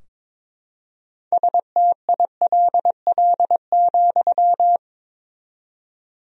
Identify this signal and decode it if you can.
Morse code, heard as a keyed tone, about 22 wpm, 700 Hz. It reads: STILL,
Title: 4x Word-Spacing - Sentences from Top 300 Words - 22wpm